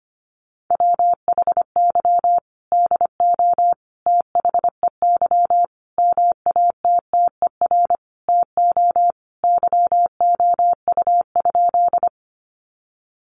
Morse code, heard as a keyed tone, about 25 wpm, 700 Hz. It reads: WHY DO THEY MATTER TO YOU?
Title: Sentences from Top 700 Words 25wpm